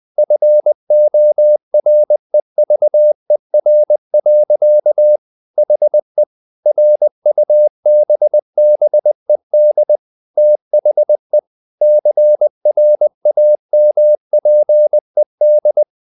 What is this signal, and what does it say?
Morse code, heard as a keyed tone, about 20 wpm, 600 Hz. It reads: FOREVER. HE RUBBED THE CRAMPED
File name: the old man and the sea / the old man and the sea part 1 20wpm